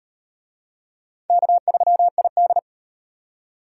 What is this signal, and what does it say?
Morse code, heard as a keyed tone, about 37 wpm, 700 Hz. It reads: K3ID